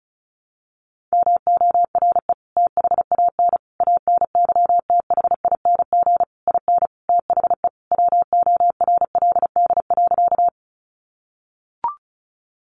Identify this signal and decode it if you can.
Morse code, heard as a keyed tone, about 34 wpm, 700 Hz. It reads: MORE THAN ANYTHING IN THE WORLD.
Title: Sentences from Top 1000 Words 35wpm